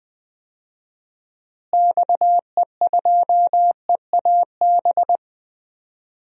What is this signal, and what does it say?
Morse code, heard as a keyed tone, about 21 wpm, 700 Hz. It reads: XE2EAB
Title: DXCC-100 Call Signs - 6 Characters - Repeated 3x - 20wpm